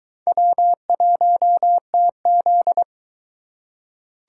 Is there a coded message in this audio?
Yes: Morse code from 0.3 to 2.8 s, heard as a keyed tone, about 22 wpm, 700 Hz. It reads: W1TZ